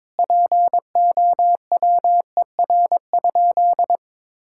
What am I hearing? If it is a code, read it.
Morse code, heard as a keyed tone, about 22 wpm, 700 Hz. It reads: POWER?
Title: Sentences from Top 1000 Words - Review Part 6 - 22wpm